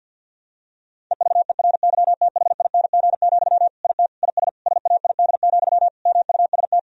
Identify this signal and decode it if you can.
Morse code, heard as a keyed tone, about 51 wpm, 700 Hz. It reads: EVERYTHING, IT IS SAID, MUST
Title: Sentences from Top 1000 Words - Review Part 1 - 50wpm